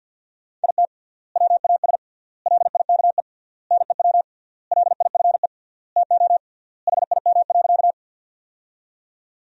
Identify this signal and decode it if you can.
Morse code, heard as a keyed tone, about 45 wpm, 700 Hz. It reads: IT WAS LIKE NEW LIFE TO HIM.
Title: 3x Word-Spacing - Sentences from Top 300 Words - 50wpm